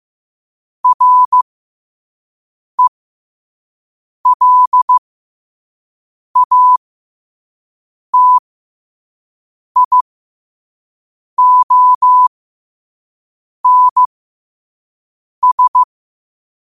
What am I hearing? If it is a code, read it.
Morse code, heard as a keyed tone, about 15 wpm, spaced as for 5 wpm, 1000 Hz. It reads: RELATIONS
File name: qotdfiles / 5wpm